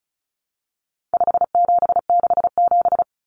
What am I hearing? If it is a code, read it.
Morse code, heard as a keyed tone, about 34 wpm, 700 Hz. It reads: 5767
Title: Common Cut Numbers - 4-Digit 35wpm